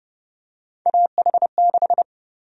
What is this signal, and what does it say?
Morse code, heard as a keyed tone, about 30 wpm, 700 Hz. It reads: AH6
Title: USA Call Sign Prefixes - 30wpm